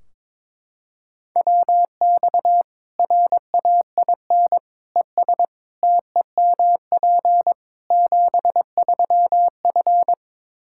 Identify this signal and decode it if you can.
Morse code, heard as a keyed tone, about 21 wpm, 700 Hz. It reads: WX RAIN ES TEMP 73F
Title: Practice QSO - Exchange 2 - Info - 22wpm